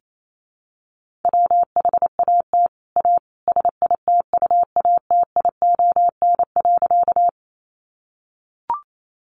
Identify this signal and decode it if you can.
Morse code, heard as a keyed tone, about 28 wpm, 700 Hz. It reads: WHAT A SITUATION.
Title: Sentences from Top 1200 Words 28wpm